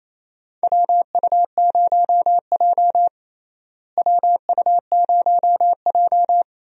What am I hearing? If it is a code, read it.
Morse code, heard as a keyed tone, about 28 wpm, 700 Hz. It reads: WU0J WU0J
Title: USA Call Signs - 4 Characters - Repeated 3x - 28wpm